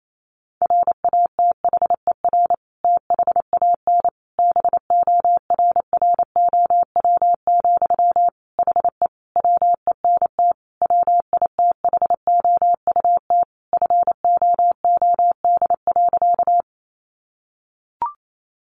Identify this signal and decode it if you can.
Morse code, heard as a keyed tone, about 28 wpm, 700 Hz. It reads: RATHER THAN BORROW, HE WENT WITHOUT FOOD.
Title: Sentences from Top 1500 Words 28wpm